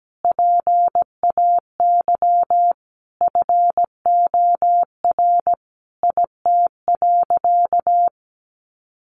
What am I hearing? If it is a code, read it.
Morse code, heard as a keyed tone, about 17 wpm, 700 Hz. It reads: PAY FOR IT.
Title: Sentences from Top 1300 Words 17wpm